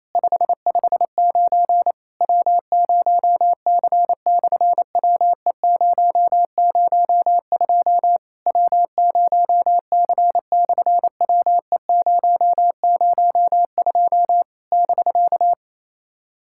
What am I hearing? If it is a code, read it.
Morse code, heard as a keyed tone, about 28 wpm, 700 Hz. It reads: 559 W0C/WE002 W0C/WE002 <BK>
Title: SOTA QSO Practice 28wpm